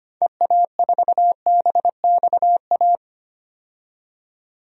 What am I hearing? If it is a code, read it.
Morse code, heard as a keyed tone, about 25 wpm, 700 Hz. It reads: EA4BXA
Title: DXCC-100 Call Signs - 6 Characters - 25wpm